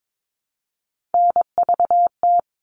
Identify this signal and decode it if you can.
Morse code, heard as a keyed tone, about 22 wpm, 700 Hz. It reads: NVT
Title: All Cut Numbers - 3-Digit 22wpm